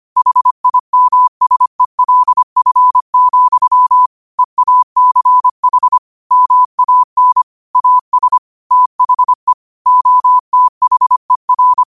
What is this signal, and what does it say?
Morse code, heard as a keyed tone, about 25 wpm, 1000 Hz. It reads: HIMSELF, EACH MAN AS THE OTHER